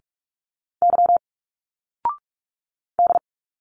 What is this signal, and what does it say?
Morse code, heard as a keyed tone, about 46 wpm, 700 Hz. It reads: Y D
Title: Single Letters 45wpm